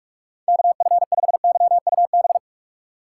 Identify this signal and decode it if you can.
Morse code, heard as a keyed tone, about 45 wpm, 700 Hz. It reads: KF5YUB
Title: K1USN SST QSO Practice - 45wpm